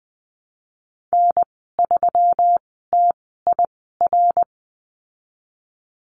Morse code, heard as a keyed tone, about 20 wpm, spaced as for 15 wpm, 700 Hz. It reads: N3TIR
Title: 154 Lesson 25g - Letter p - Call Signs 20wpm